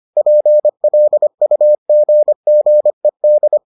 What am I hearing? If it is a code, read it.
Morse code, heard as a keyed tone, about 25 wpm, 600 Hz. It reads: PLUGGED